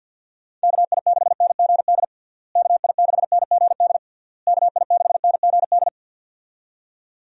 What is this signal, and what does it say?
Morse code, heard as a keyed tone, about 52 wpm, 700 Hz. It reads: KI6NGD KI6NGD KI6NGD
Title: USA Call Signs - 6 Characters - Repeated 3x - 50wpm